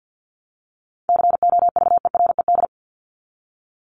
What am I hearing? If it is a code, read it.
Morse code, heard as a keyed tone, about 49 wpm, 700 Hz. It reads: COVERED
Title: ED Suffix - Top 100 Words with Base Words 50wpm